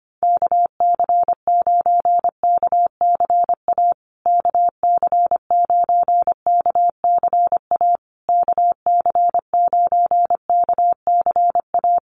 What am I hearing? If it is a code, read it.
Morse code, heard as a keyed tone, about 25 wpm, 700 Hz. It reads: KC9KCA KC9KCA KC9KCA
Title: USA Call Signs Repeated 3x 25wpm